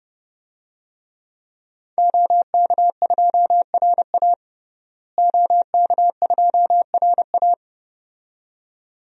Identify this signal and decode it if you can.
Morse code, heard as a keyed tone, about 32 wpm, 700 Hz. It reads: OK2RA OK2RA
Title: DXCC-100 Call Signs - 5 Characters - Repeated 2x - 30wpm